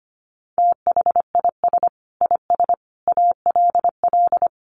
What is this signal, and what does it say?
Morse code, heard as a keyed tone, about 25 wpm, 700 Hz. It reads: THIS IS ALL
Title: Sets of 5 Words 25wpm